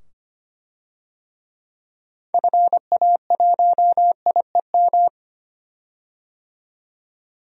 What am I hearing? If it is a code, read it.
Morse code, heard as a keyed tone, about 25 wpm, 700 Hz. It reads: FA1IEM
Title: DXCC-100 Call Signs - 6 Characters - 25wpm